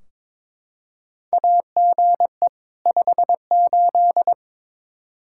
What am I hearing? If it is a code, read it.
Morse code, heard as a keyed tone, about 22 wpm, 700 Hz. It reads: AGE 58
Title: Practice QSO - Exchange 3 - More Info - 22wpm